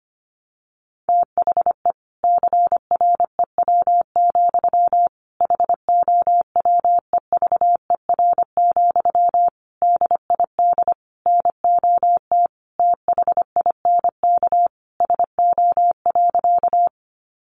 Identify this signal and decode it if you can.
Morse code, heard as a keyed tone, about 25 wpm, 700 Hz. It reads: THE CREW, HOWEVER, DID NOT THINK SO.
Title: Sentences from Top 1500 Words 25wpm